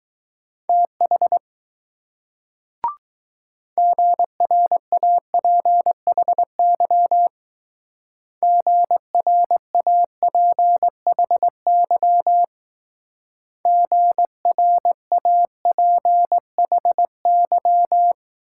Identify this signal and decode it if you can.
Morse code, heard as a keyed tone, about 20 wpm, 700 Hz. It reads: TH GRAPHY GRAPHY GRAPHY